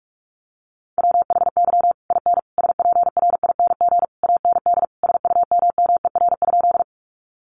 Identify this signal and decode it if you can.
Morse code, heard as a keyed tone, about 44 wpm, 700 Hz. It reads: WHY IN SPRING AND SUMMER?
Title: Sentences from Top 1200 Words 45wpm